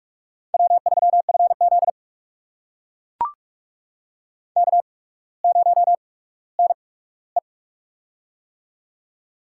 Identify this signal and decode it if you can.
Morse code, heard as a keyed tone, about 42 wpm, spaced as for 19 wpm, 690 Hz. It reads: W3FZ K0NE
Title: USA Call Signs - 4 Characters - 8x Character Spacing - 45wpm